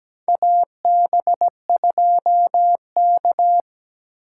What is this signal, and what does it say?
Morse code, heard as a keyed tone, about 17 wpm, 700 Hz. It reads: AB2K
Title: USA Call Signs - 4 Characters - 3x Character Spacing - 17wpm